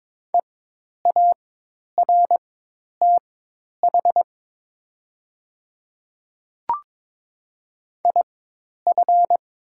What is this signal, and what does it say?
Morse code, heard as a keyed tone, about 22 wpm, spaced as for 10 wpm, 700 Hz. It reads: EARTH IF
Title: 4x Character Spacing - Top 200 Words - 22wpm